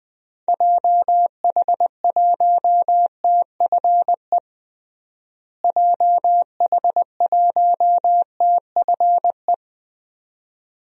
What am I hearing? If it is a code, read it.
Morse code, heard as a keyed tone, about 20 wpm, 700 Hz. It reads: JH1TFE JH1TFE
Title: DXCC-100 Call Signs - 6 Characters - Repeated 2x - 20wpm